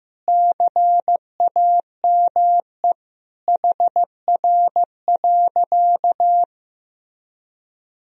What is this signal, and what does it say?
Morse code, heard as a keyed tone, about 15 wpm, 700 Hz. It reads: CAME HR.
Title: Sentences from Top 500 Words and Abbreviations 15wpm